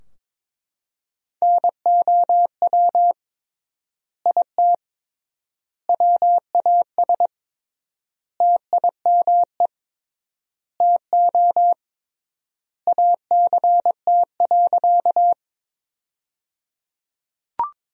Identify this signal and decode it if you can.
Morse code, heard as a keyed tone, about 22 wpm, 700 Hz. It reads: NOW IT WAS TIME TO ACT.
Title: 3x Word-Spacing - Sentences from Top 200 Words - 22wpm